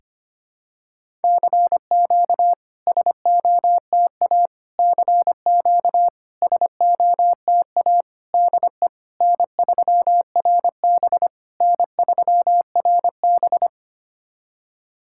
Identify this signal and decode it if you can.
Morse code, heard as a keyed tone, about 25 wpm, 700 Hz. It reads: CQ SOTA CQ SOTA DE N3RB N3RB
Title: SOTA QSO Practice 25wpm